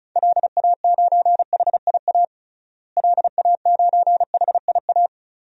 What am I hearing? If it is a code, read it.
Morse code, heard as a keyed tone, about 34 wpm, 700 Hz. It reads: LA9HIA LA9HIA
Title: DXCC-100 Call Signs - 6 Characters - Repeated 3x - 35wpm